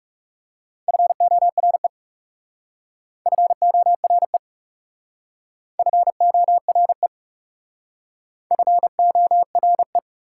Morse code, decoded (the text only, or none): FORE FORE FORE FORE